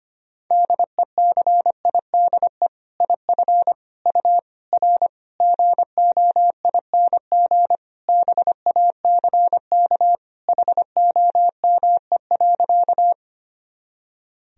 Morse code, heard as a keyed tone, about 25 wpm, 700 Hz. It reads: DECIDE IF U R GOING BACK HOME.
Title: Sentences from Top 500 Words and Abbreviations 25wpm